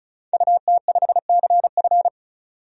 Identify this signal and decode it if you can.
Morse code, heard as a keyed tone, about 36 wpm, 700 Hz. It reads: UT5CF
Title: DXCC-100 Call Signs - 5 Characters - 35wpm